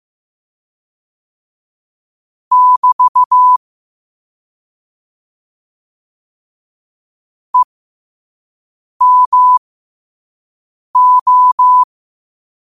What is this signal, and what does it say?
Morse code, heard as a keyed tone, about 15 wpm, spaced as for 5 wpm, 1000 Hz. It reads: <BT> EMO
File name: qotdfiles / 5wpm